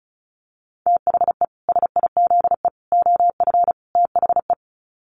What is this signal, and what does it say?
Morse code, heard as a keyed tone, about 34 wpm, 700 Hz. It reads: THE SIZE OF THE
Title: Sets of 4 Words 35wpm